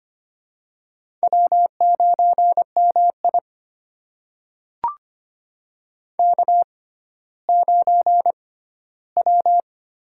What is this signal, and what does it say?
Morse code, heard as a keyed tone, about 25 wpm, spaced as for 14 wpm, 700 Hz. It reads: W9MI K9W
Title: USA Call Signs - 4 Characters - 6x Character Spacing - 25wpm